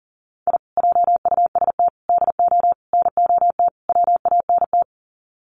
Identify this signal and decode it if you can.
Morse code, heard as a keyed tone, about 40 wpm, 700 Hz. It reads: I JUST DO NOT WANT
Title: Sets of 5 Words Encore2 40wpm